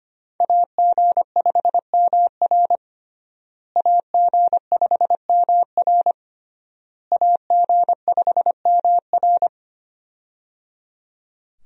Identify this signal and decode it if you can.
Morse code, heard as a keyed tone, about 25 wpm, 700 Hz. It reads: AG5MR AG5MR AG5MR
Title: USA Call Signs - 5 Characters - Repeated 3x - 25wpm